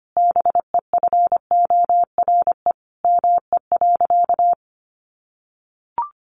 Morse code, heard as a keyed tone, about 25 wpm, 700 Hz. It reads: BEFORE ME.
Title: Sentences from Top 1300 Words 25wpm